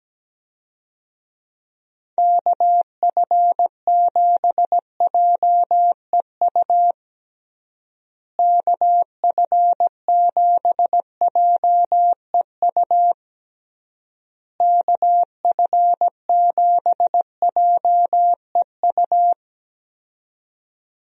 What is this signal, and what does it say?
Morse code, heard as a keyed tone, about 17 wpm, 700 Hz. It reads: KF7JEU KF7JEU KF7JEU
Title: USA Call Signs - 6 Characters - Repeated 3x - 17wpm